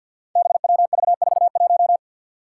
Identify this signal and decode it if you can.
Morse code, heard as a keyed tone, about 52 wpm, 700 Hz. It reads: BWV41
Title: Five Challenging Letter-Number Combinations 50wpm